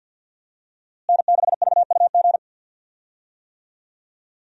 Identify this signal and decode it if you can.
Morse code, heard as a keyed tone, about 53 wpm, 700 Hz. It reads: N6VUG
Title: USA Call Signs - 5 Characters - Repeated 2x - 50wpm